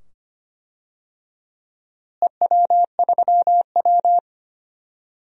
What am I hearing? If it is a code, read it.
Morse code, heard as a keyed tone, about 25 wpm, 700 Hz. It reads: EW3W
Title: DXCC-100 Call Signs - 4 Characters - Repeated 2x - 25wpm